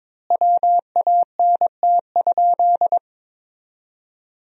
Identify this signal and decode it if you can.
Morse code, heard as a keyed tone, about 22 wpm, 700 Hz. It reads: WANT?
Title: 4x Word-Spacing - Sentences from Top 200 Words - 22wpm